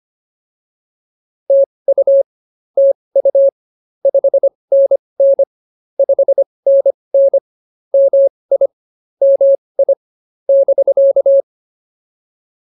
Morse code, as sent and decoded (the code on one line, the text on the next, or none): - ..- / - ..- / ..... -. -. / ..... -. -. / -- .. / -- .. / -...-.-
TU TU 5NN 5NN MI MI <BK>